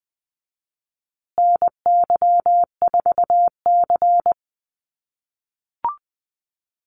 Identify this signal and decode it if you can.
Morse code, heard as a keyed tone, about 20 wpm, 700 Hz. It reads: NY4C
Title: USA Call Signs Repeated 3x 20wpm